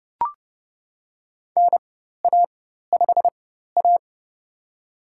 Morse code, heard as a keyed tone, about 32 wpm, spaced as for 14 wpm, 700 Hz. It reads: NA5A
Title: USA Call Signs - 4 Characters - 4x Character Spacing - 30wpm